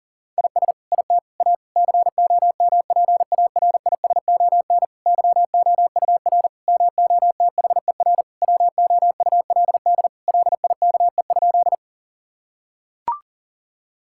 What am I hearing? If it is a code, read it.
Morse code, heard as a keyed tone, about 40 wpm, 700 Hz. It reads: IS IT A COMPARISON YOUR MOTHER WOULD LIKE?
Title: Sentences from Top 1600 Words 40wpm